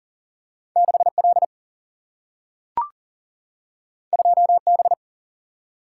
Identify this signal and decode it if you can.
Morse code, heard as a keyed tone, about 40 wpm, 700 Hz. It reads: 6 L 2 B
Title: Single-Letter Single-Number 40wpm